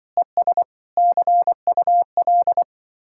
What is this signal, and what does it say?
Morse code, heard as a keyed tone, about 24 wpm, 700 Hz. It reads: ES CUL
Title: Speed-Racing Common QSO Phrases Encore - 20wpm